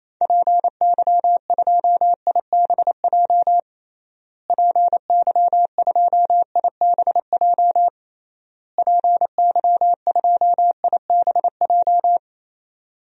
Morse code, heard as a keyed tone, about 28 wpm, 700 Hz. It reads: PY2IBJ PY2IBJ PY2IBJ
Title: DXCC-100 Call Signs - 6 Characters - Repeated 3x - 28wpm